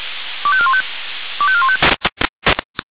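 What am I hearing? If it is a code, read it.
Five-tone selective call (ZVEI), heard as quick runs of five tones, tones of about 70 ms. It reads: 25415 25415
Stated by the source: ZVEI five-tone selective call received on an SDR.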